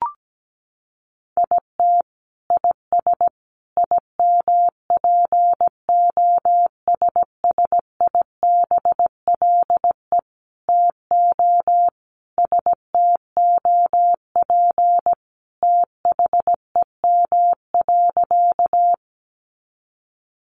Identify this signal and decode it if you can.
Morse code, heard as a keyed tone, about 17 wpm, 700 Hz. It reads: IT IS IMPOSSIBLE TO STOP THEM.
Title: Sentences from Top 1100 Words 17wpm